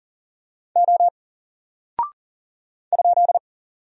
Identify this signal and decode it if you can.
Morse code, heard as a keyed tone, about 40 wpm, 700 Hz. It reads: O ?
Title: Single Character (Letter:Number:Punctuation:Prosign) 40wpm